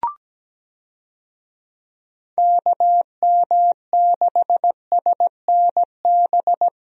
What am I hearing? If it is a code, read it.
Morse code, heard as a keyed tone, about 17 wpm, 700 Hz. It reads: KM6SNB
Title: USA Call Signs - 6 Characters - Repeated 2x - 17wpm